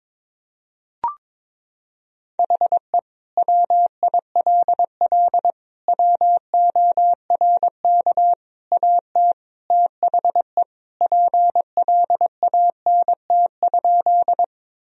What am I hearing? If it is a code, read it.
Morse code, heard as a keyed tone, about 22 wpm, 700 Hz. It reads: HE WILL WORK AT THE PLANT?
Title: Sentences from Top 1000 Words - Review Part 1 - 22wpm